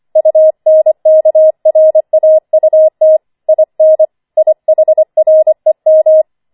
Morse code, heard as a keyed tone, about 25 wpm, 630 Hz. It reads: UNKRAUT IN IHREM
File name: 25wpm